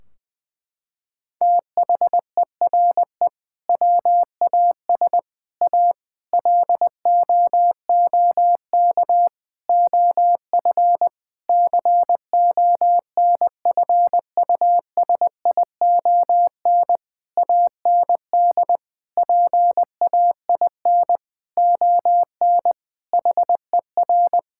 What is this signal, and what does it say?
Morse code, heard as a keyed tone, about 20 wpm, 700 Hz. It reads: THERE WAS A LOOK OF CONFUSION AND PAIN ON HER
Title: Sentences from Top 1600 Words 20wpm